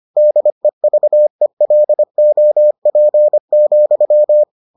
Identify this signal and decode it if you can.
Morse code, heard as a keyed tone, about 25 wpm, 600 Hz. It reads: DEVELOP,